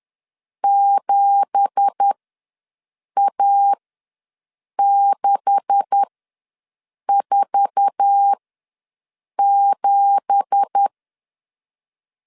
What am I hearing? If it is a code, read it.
Morse code, heard as a keyed tone, about 10 wpm, spaced as for 6 wpm, 800 Hz. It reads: 7A647
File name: LEZIONI / 9 2867 ws 10wpm 4215